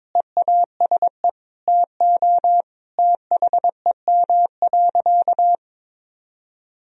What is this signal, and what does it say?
Morse code, heard as a keyed tone, about 22 wpm, 700 Hz. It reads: EASE TO THEM.
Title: Sentences from Top 1000 Words - Review Part 1 - 22wpm